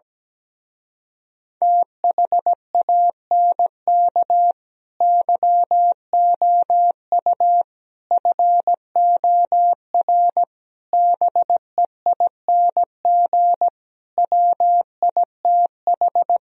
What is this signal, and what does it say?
Morse code, heard as a keyed tone, about 17 wpm, 700 Hz. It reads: THANK YOU FOR BEING WITH
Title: Speed-Racing - Sets of 5 Words 17wpm